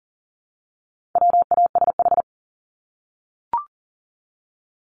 Morse code, heard as a keyed tone, about 40 wpm, 700 Hz. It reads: WASH
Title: Top 1751-2000 Words 40wpm